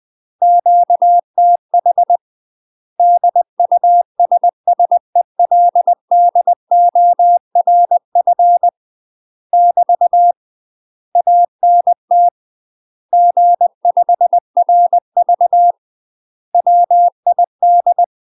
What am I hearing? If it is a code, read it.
Morse code, heard as a keyed tone, about 20 wpm, 700 Hz. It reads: QTH DUSSELDORF <BT> ANT G5RV WID
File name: UK MASTER QSOs TRAINING / 20WPM / UK MASTER QSOs 0026